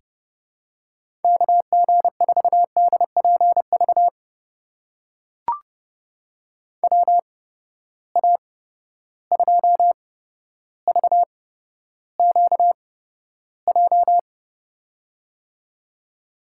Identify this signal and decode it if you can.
Morse code, heard as a keyed tone, about 30 wpm, spaced as for 13 wpm, 700 Hz. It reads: KG4DPV WA2VQJ
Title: USA Call Signs - 6 Characters - 8x Character Spacing - 30wpm